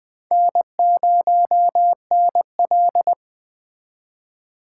Morse code, heard as a keyed tone, about 22 wpm, 700 Hz. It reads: N0NL